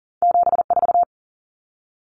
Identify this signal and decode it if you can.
Morse code, heard as a keyed tone, about 40 wpm, 700 Hz. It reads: 74
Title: Common Cut Numbers - 2-Digit 40wpm